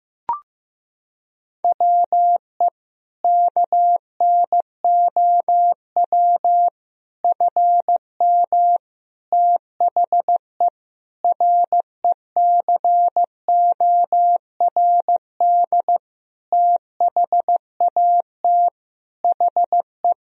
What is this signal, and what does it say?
Morse code, heard as a keyed tone, about 15 wpm, 700 Hz. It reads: WE KNOW FM THE RECORD THAT HE
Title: Sentences from Top 500 Words and Abbreviations 15wpm